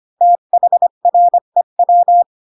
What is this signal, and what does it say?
Morse code, heard as a keyed tone, about 25 wpm, 700 Hz. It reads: THREW
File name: Pinocchio / Pinocchio ch4 20 25 ik1hge